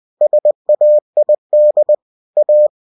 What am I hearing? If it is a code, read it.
Morse code, heard as a keyed tone, about 20 wpm, 600 Hz. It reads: SAID A
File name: cw-books / winnie-the-pooh / winnie-20wpm-page-0013